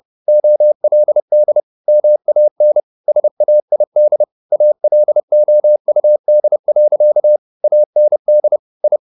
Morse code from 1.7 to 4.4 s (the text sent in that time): MAN SAID